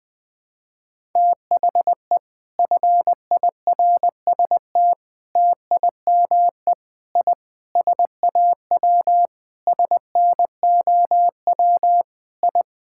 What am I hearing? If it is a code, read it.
Morse code, heard as a keyed tone, about 19 wpm, 700 Hz. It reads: THE FIRST TIME I SAW SNOW I
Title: Sentences with 4-Word Phrases - Spoken Beginning - 20wpm